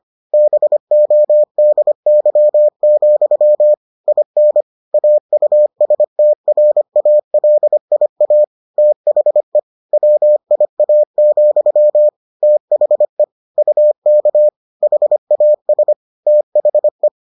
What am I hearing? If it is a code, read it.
Morse code, heard as a keyed tone, about 25 wpm, 600 Hz. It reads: BODY, IN AUSTRALIA THE WIA, THE UK HAS THE